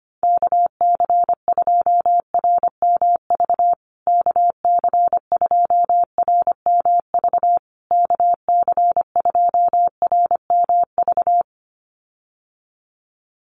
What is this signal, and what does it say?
Morse code, heard as a keyed tone, about 25 wpm, 700 Hz. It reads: KC2RMV KC2RMV KC2RMV
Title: USA Call Signs Repeated 3x 25wpm